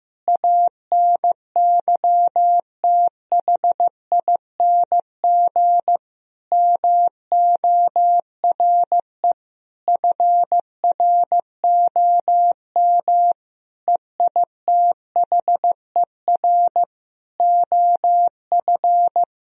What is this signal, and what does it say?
Morse code, heard as a keyed tone, about 15 wpm, 700 Hz. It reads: ANYTHING MORE FROM EITHER OF
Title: Sentences from Top 1000 Words - Review Part 4 - 15wpm